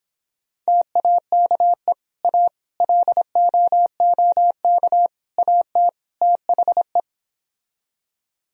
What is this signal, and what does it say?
Morse code, heard as a keyed tone, about 26 wpm, 700 Hz. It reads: TAKE A LOOK AT THE